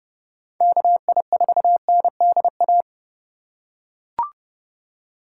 Morse code, heard as a keyed tone, about 30 wpm, 700 Hz. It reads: KI4NDA
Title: USA Call Signs - 6 Characters - 4x Character Spacing - 30wpm